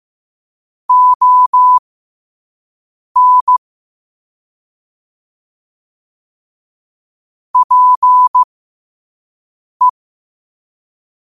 Morse code, heard as a keyed tone, about 15 wpm, spaced as for 5 wpm, 1000 Hz. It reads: ON PE
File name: qotdfiles / 5wpm